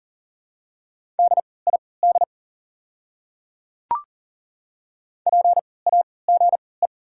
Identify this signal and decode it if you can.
Morse code, heard as a keyed tone, about 40 wpm, spaced as for 23 wpm, 690 Hz. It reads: DID PAGE
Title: 3x Character Spacing - Top 500 Words - 40wpm